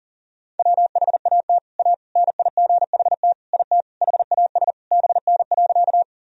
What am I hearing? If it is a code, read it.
Morse code, heard as a keyed tone, about 40 wpm, 700 Hz. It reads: WHAT A NIGHT IT HAS BN.
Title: Sentences from Top 300 Words and Abbreviations 40wpm